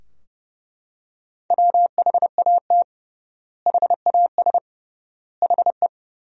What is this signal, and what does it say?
Morse code, heard as a keyed tone, about 31 wpm, 700 Hz. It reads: WHAT HAS HE
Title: 3x Word-Spacing - Sentences from Top 500 Words - 30wpm